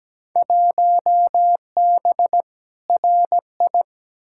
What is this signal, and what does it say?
Morse code, heard as a keyed tone, about 17 wpm, 700 Hz. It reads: 1B RI
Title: Field Day Exchanges Encore - 17wpm